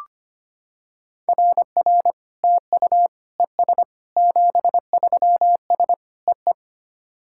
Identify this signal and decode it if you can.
Morse code, heard as a keyed tone, about 25 wpm, 690 Hz. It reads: RR TU ES 73S EE
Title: SOTA QSO Practice 25wpm